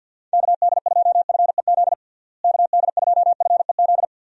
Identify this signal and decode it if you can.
Morse code, heard as a keyed tone, about 48 wpm, 700 Hz. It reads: KD2FEZ KD2FEZ
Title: USA Call Signs - 6 Characters - Repeated 2x - 50wpm